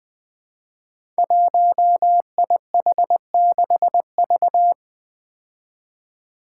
Morse code, heard as a keyed tone, about 20 wpm, 700 Hz. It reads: 1IH6V
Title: Five Challenging Letter-Number Combinations 20wpm